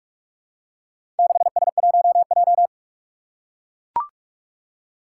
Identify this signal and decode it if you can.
Morse code, heard as a keyed tone, about 44 wpm, 700 Hz. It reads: 6S1J
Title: Four Challenging Letter-Number Combinations 45wpm